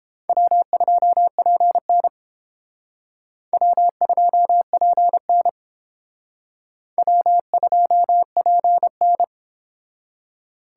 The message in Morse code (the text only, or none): W2PN W2PN W2PN